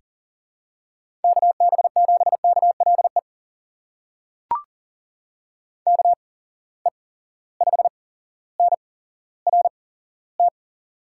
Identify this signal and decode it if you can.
Morse code, heard as a keyed tone, about 41 wpm, spaced as for 17 wpm, 700 Hz. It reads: KB7KLE KE5NRT
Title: USA Call Signs - 6 Characters - 8x Character Spacing - 40wpm